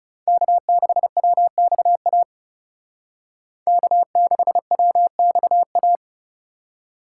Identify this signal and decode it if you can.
Morse code, heard as a keyed tone, about 38 wpm, 700 Hz. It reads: K6WXA K6WXA